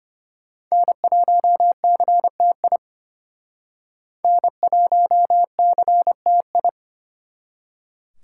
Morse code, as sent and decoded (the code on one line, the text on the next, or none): -. .---- -.-. - .. / -. .---- -.-. - ..
N1CTI N1CTI